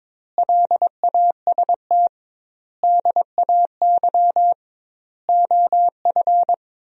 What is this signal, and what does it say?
Morse code, heard as a keyed tone, about 22 wpm, 700 Hz. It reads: LAST DAY OF
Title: 2x Word-Spacing - Sentences from Top 300 Words - 22wpm